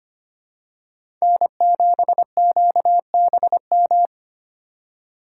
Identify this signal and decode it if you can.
Morse code, heard as a keyed tone, about 25 wpm, 700 Hz. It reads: N7QBM
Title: USA Call Signs - 5 Characters - 2x Character Spacing - 25wpm